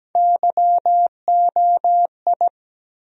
Morse code, heard as a keyed tone, about 17 wpm, 700 Hz. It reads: YOI
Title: USA Call Sign Postfixes - 3 Letters 17wpm